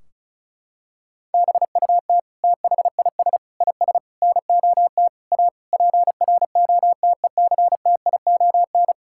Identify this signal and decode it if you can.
Morse code, heard as a keyed tone, about 36 wpm, 700 Hz. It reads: BUT THIS IS NOT A PROTECTION
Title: Sentences from Top 1600 Words 35wpm